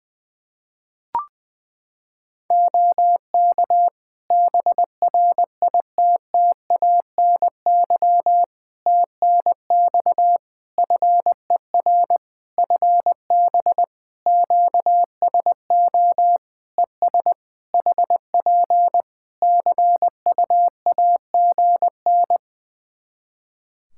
Morse code, heard as a keyed tone, about 21 wpm, 700 Hz. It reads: OK BRITTANY TNX FER FB QSO ES HP CUAGN
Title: Practice QSO - Exchange 4 - Ending - 20wpm